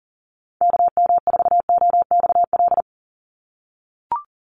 Morse code, heard as a keyed tone, about 40 wpm, 700 Hz. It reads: KM4OXL
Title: USA Call Signs 40wpm